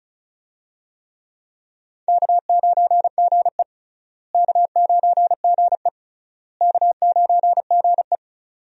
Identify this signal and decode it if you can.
Morse code, heard as a keyed tone, about 36 wpm, 700 Hz. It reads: K9GE K9GE K9GE
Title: USA Call Signs - 4 Characters - Repeated 3x - 35wpm